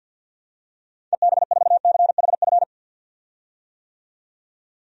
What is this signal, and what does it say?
Morse code, heard as a keyed tone, about 51 wpm, 700 Hz. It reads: EB4CHF
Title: DXCC-100 Call Signs - 6 Characters - 50wpm